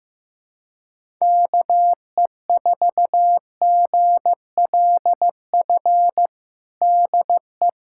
Morse code, heard as a keyed tone, about 15 wpm, 700 Hz. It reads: KE4GLF DE
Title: USA Callsign DE Callsign 15wpm